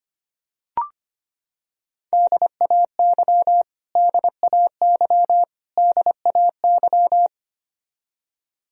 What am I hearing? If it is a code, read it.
Morse code, heard as a keyed tone, about 25 wpm, 700 Hz. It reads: DAY DAY DAY
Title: Top 100 Words Repeated 25wpm